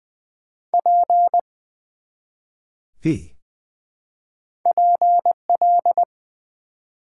Morse code, heard as a keyed tone, about 20 wpm, 700 Hz. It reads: P PL
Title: Build A Word - 1 Letter At A Time - 4-Letter Words - 20wpm